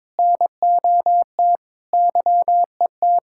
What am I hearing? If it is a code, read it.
Morse code, heard as a keyed tone, about 22 wpm, 700 Hz. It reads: NOT YET
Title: Sentences from Top 1600 Words 22wpm